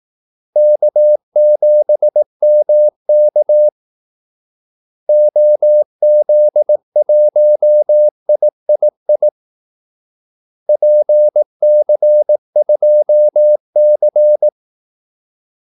Morse code, heard as a keyed tone, about 18 wpm, 600 Hz. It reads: K7MK OZ1III PC2C
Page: Kaldesignaler 18 wpm